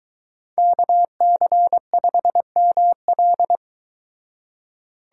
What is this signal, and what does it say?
Morse code, heard as a keyed tone, about 24 wpm, 700 Hz. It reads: KC5ML